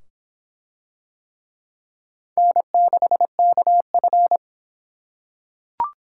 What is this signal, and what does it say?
Morse code, heard as a keyed tone, about 23 wpm, 700 Hz. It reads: N6KF